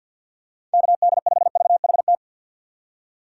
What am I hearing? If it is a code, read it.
Morse code, heard as a keyed tone, about 51 wpm, 700 Hz. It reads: KD5VHT
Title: USA Call Signs - 6 Characters - 3x Character Spacing - 50wpm